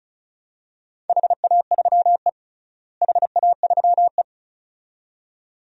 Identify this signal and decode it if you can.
Morse code, heard as a keyed tone, about 34 wpm, 700 Hz. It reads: HA3E HA3E
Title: DXCC-100 Call Signs - 4 Characters - Repeated 2x - 35wpm